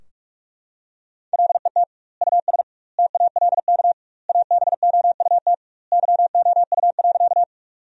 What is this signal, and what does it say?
Morse code, heard as a keyed tone, about 45 wpm, 700 Hz. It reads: LET US TALK ABOUT YOU.
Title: 2x Word-Spacing - Sentences from Top 400 Words - 45wpm